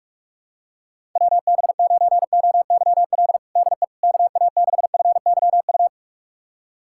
Morse code, heard as a keyed tone, about 46 wpm, 700 Hz. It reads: WB9OYL DE KA6FYU
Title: USA Callsign DE Callsign - 6 Characters - 45wpm